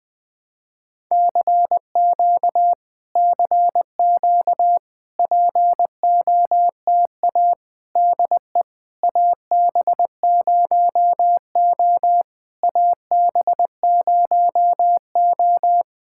Morse code, heard as a keyed tone, about 20 wpm, 700 Hz. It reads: CQ CQ POTA DE AB0O AB0O
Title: POTA QSO Practice - 20wpm